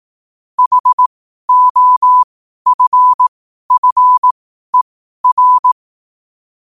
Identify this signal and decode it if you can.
Morse code, heard as a keyed tone, about 18 wpm, spaced as for 12 wpm, 1000 Hz. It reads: HOFFER
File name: qotdfiles / 12wpm